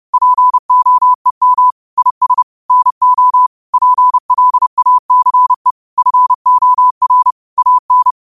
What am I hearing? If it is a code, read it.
Morse code, heard as a keyed tone, about 30 wpm, 1000 Hz. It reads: POEM IS NO PLACE FOR AN
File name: 30wpm